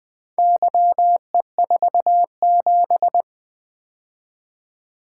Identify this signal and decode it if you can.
Morse code, heard as a keyed tone, about 20 wpm, 700 Hz. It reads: YE47
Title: Four Letter-Number Combinations - 20wpm